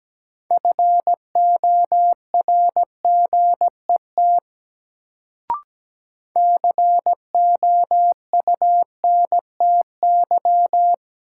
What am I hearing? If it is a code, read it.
Morse code, heard as a keyed tone, about 17 wpm, 700 Hz. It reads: FORGET COUNTY
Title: Sentences from Top 1200 Words - New Word Drill Alt - 17wpm